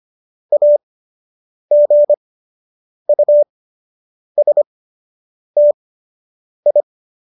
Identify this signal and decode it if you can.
Morse code, heard as a keyed tone, about 25 wpm, spaced as for 8 wpm, 590 Hz. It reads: AGUSTI